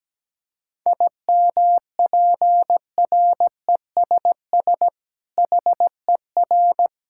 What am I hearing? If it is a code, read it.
Morse code, heard as a keyed tone, about 17 wpm, 700 Hz. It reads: IMPRESS HER
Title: Sentences from Top 1600 Words 17wpm